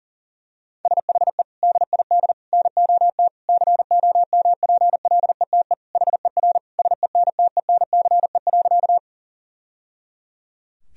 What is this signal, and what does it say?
Morse code, heard as a keyed tone, about 40 wpm, 700 Hz. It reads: SHE DID NOT COMPLETE HER SENTENCE.
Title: Sentences from Top 1000 Words - Review Part 2 - 40wpm